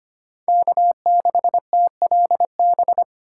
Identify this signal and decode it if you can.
Morse code, heard as a keyed tone, about 25 wpm, 700 Hz. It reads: K6TLB
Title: USA Call Signs - 5 Characters - 6x Character Spacing - 25wpm